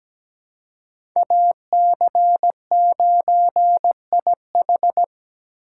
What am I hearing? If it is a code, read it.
Morse code, heard as a keyed tone, about 17 wpm, 700 Hz. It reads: AC9IH
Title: USA Call Signs - 5 Characters - 3x Character Spacing - 17wpm